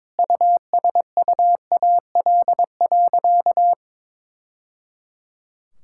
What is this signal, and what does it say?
Morse code, heard as a keyed tone, about 22 wpm, 700 Hz. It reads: USUAL.
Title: Sentences from Top 1000 Words - Review Part 6 - 22wpm